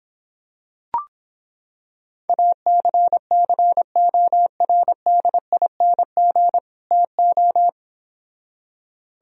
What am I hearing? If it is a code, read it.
Morse code, heard as a keyed tone, about 26 wpm, 700 Hz. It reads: ACCORDING TO